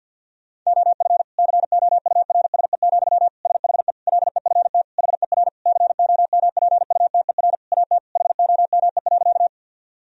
Morse code, heard as a keyed tone, about 50 wpm, 700 Hz. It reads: OF COURSE, SHE LEFT HER COMPUTER AT HOME.
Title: Sentences from Top 1000 Words - Review Part 3 - 50wpm